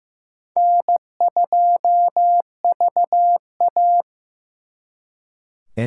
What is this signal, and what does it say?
Morse code, heard as a keyed tone, about 18 wpm, 700 Hz. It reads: N2VA
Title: Speed Racing - USA 1x2 Call Signs 15wpm